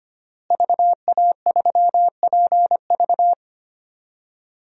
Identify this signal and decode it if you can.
Morse code, heard as a keyed tone, about 25 wpm, 700 Hz. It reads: VA3PV
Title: DXCC-100 Call Signs - 5 Characters - Repeated 3x - 25wpm